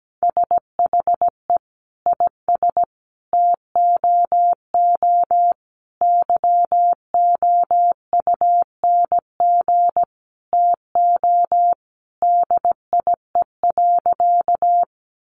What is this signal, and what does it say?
Morse code, heard as a keyed tone, about 17 wpm, 700 Hz. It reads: SHE IS TOO YOUNG TO DIE.
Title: Sentences from Top 1300 Words 17wpm